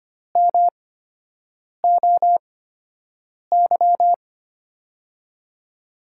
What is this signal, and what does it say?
Morse code, heard as a keyed tone, about 25 wpm, spaced as for 7 wpm, 700 Hz. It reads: MOY
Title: USA Call Signs - 6 Characters - 8x Character Spacing - 25wpm